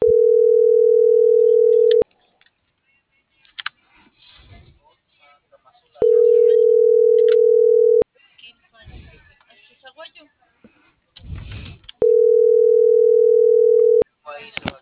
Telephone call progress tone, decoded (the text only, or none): ringback tone